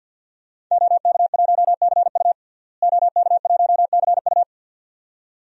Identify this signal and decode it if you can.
Morse code, heard as a keyed tone, about 51 wpm, 700 Hz. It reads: OK1CU OK1CU
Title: DXCC-100 Call Signs - 5 Characters - Repeated 2x - 50wpm